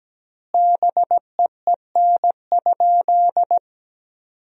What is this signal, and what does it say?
Morse code, heard as a keyed tone, about 17 wpm, 700 Hz. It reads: BEEN?
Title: 3x Word-Spacing - Sentences from Top 200 Words - 17wpm